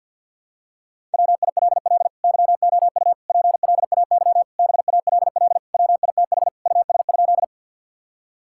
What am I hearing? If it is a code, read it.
Morse code, heard as a keyed tone, about 47 wpm, 700 Hz. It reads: WILL YOU PLAY BALL WITH US?
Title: Sentences from Top 1000 Words - Review Part 6 - 50wpm